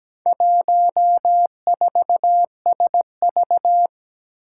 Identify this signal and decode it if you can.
Morse code, heard as a keyed tone, about 17 wpm, 700 Hz. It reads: 14SV
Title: Four Challenging Letter-Number Combinations 17wpm